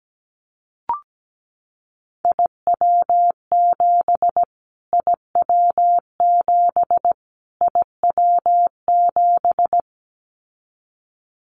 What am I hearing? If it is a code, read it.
Morse code, heard as a keyed tone, about 17 wpm, 700 Hz. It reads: IW7 IW7 IW7
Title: Callsign Prefixes for Top 25 DXCC - Repeated 3x - 17wpm